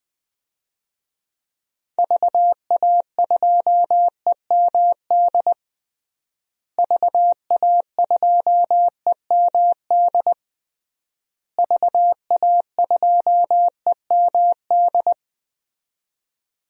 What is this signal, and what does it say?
Morse code, heard as a keyed tone, about 20 wpm, 700 Hz. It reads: VA2EMD VA2EMD VA2EMD
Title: DXCC-100 Call Signs - 6 Characters - Repeated 3x - 20wpm